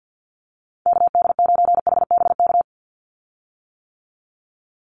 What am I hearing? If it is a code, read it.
Morse code, heard as a keyed tone, about 50 wpm, 700 Hz. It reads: KD9HBK
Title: USA Call Signs 50wpm